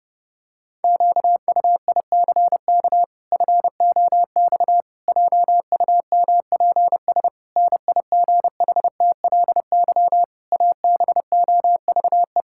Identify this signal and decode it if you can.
Morse code, heard as a keyed tone, about 30 wpm, 700 Hz. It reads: QUICK FOX JUMPS NIGHTLY ABOVE
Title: Pangram Challenge v2 - 30wpm